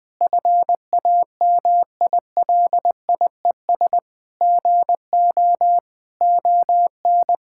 Morse code, heard as a keyed tone, about 21 wpm, 700 Hz. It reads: FAMILIES GO ON
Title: Difficult Sentences with Hints - 20wpm